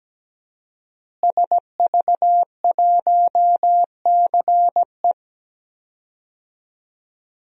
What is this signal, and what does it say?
Morse code, heard as a keyed tone, about 17 wpm, 700 Hz. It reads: SV1CE
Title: DXCC-100 Call Signs - 5 Characters - Repeated 2x - 17wpm